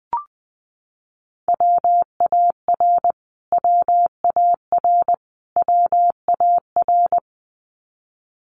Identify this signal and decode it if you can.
Morse code, heard as a keyed tone, about 19 wpm, 700 Hz. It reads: WAR WAR WAR
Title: Top 500 Words Repeated 20wpm